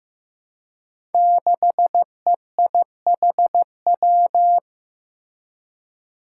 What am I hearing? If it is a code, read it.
Morse code, heard as a keyed tone, about 15 wpm, 700 Hz. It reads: 6EIHW
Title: Five Challenging Letter-Number Combinations 15wpm